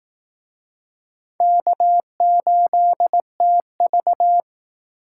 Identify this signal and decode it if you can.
Morse code, heard as a keyed tone, about 18 wpm, 700 Hz. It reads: K8TV
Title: Speed Racing - USA 1x2 Call Signs 15wpm